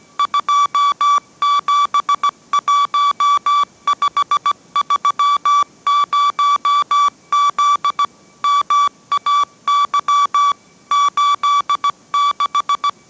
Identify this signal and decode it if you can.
Morse code, heard as a keyed tone, about 17 wpm, 1150 Hz. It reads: 271530Z MAY 86